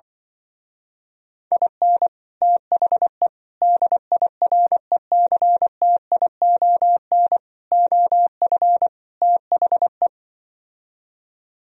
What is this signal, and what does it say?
Morse code, heard as a keyed tone, about 24 wpm, 700 Hz. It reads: IN THE DIRECTION OF THE
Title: Speed-Racing - Sets of 5 Words 20wpm